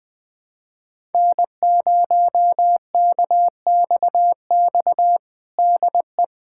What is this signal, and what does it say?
Morse code, heard as a keyed tone, about 20 wpm, 700 Hz. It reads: N0KXX DE
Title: USA Callsign DE Callsign - 5 Characters - 20wpm